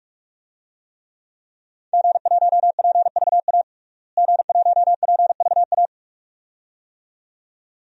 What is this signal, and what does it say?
Morse code, heard as a keyed tone, about 46 wpm, 700 Hz. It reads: G1PVA G1PVA
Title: DXCC-100 Call Signs - 5 Characters - Repeated 2x - 45wpm